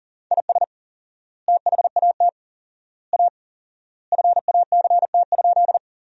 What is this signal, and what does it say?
Morse code, heard as a keyed tone, about 40 wpm, 700 Hz. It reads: IS THAT A FACT?
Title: 4x Word-Spacing - Sentences from Top 500 Words - 40wpm